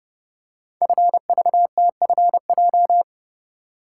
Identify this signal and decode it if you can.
Morse code, heard as a keyed tone, about 30 wpm, 690 Hz. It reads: FVTFJ
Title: Code Groups - 5 Letters - 30wpm